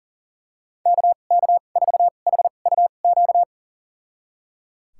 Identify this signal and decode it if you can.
Morse code, heard as a keyed tone, about 37 wpm, spaced as for 28 wpm, 700 Hz. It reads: KK4HUQ
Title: USA Call Signs - 6 Characters - 2x Character Spacing - 40wpm